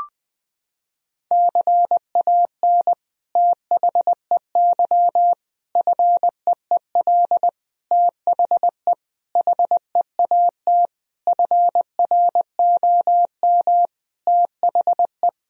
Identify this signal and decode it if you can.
Morse code, heard as a keyed tone, about 20 wpm, 700 Hz. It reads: CAN THEY FEEL THE HEAT FROM THE
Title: Sentences from Top 1000 Words - Review Part 3 - 20wpm